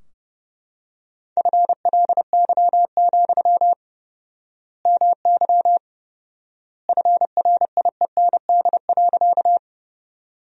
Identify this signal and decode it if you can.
Morse code, heard as a keyed tone, about 30 wpm, 700 Hz. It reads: FLY, MY FRIEND.
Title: 4x Word-Spacing - Sentences from Top 500 Words - 30wpm